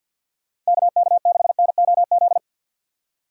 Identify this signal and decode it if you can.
Morse code, heard as a keyed tone, about 62 wpm, 700 Hz. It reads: KK6NOZ
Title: Speed Racing - USA 2x3 Call Signs 50wpm